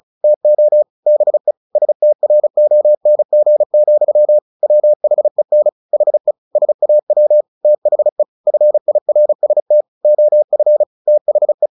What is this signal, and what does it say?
Morse code, heard as a keyed tone, about 34 wpm, 600 Hz. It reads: TO BE STRONG, WHEN HE SAW THE FIRST OF THE
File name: the old man and the sea 2 35wpm